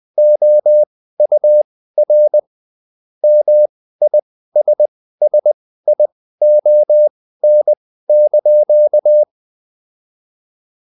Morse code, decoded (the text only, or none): OUR MISSION)